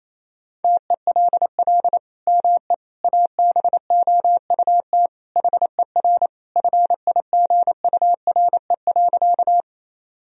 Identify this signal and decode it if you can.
Morse code, heard as a keyed tone, about 28 wpm, 700 Hz. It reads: TELL ME ABOUT HER FIGURE.
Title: Sentences from Top 1000 Words - Review Part 2 - 28wpm